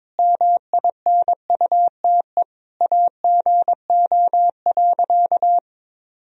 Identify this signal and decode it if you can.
Morse code, heard as a keyed tone, about 21 wpm, 700 Hz. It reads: MINUTE AGO.
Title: Sentences from Top 1000 Words - Review Part 3 - 22wpm